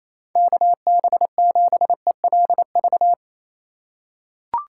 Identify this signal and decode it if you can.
Morse code, heard as a keyed tone, about 28 wpm, 700 Hz. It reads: KB7ELV
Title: USA Call Signs - 6 Characters - 2x Character Spacing - 28wpm